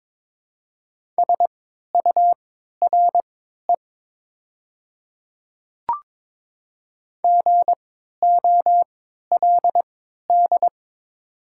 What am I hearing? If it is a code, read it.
Morse code, heard as a keyed tone, about 22 wpm, spaced as for 12 wpm, 700 Hz. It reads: SURE GOLD
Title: 3x Character Spacing - Top 500 Words - 22wpm